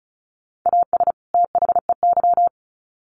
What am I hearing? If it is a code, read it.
Morse code, heard as a keyed tone, about 34 wpm, 700 Hz. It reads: AS THEY
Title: Sets of 2 Words 35wpm